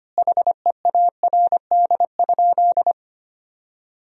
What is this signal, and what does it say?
Morse code, heard as a keyed tone, about 25 wpm, 700 Hz. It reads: HEARD?
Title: 2x Word-Spacing - Sentences from Top 400 Words - 25wpm